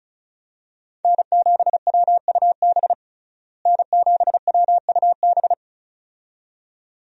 Morse code, heard as a keyed tone, about 34 wpm, 700 Hz. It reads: N7WUB N7WUB
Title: USA Call Signs - 5 Characters - Repeated 2x - 35wpm